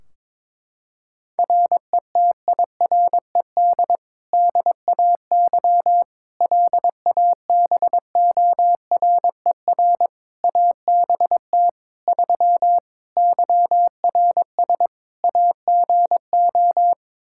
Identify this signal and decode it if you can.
Morse code, heard as a keyed tone, about 22 wpm, 700 Hz. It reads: RETIRED DAY LABORER ABT 3 YRS AGO
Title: Practice QSO - Exchange 3 - More Info - 22wpm